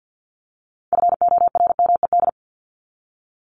Morse code, heard as a keyed tone, about 48 wpm, 700 Hz. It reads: FORMED
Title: ED Suffix - Top 100 Words with Base Words 50wpm